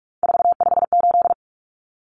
Morse code, heard as a keyed tone, about 44 wpm, 700 Hz. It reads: 458
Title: Common Cut Numbers - 3-Digit 45wpm